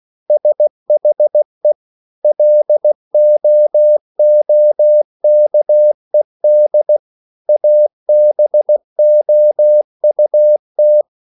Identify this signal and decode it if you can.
Morse code, heard as a keyed tone, about 16 wpm, 600 Hz. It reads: SHE LOOKED ABOUT